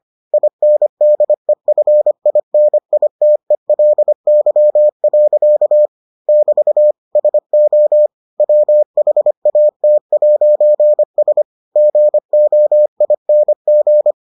Morse code, heard as a keyed tone, about 25 wpm, 600 Hz. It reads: INDEFINITELY. <BT> SO WHAT'S GOING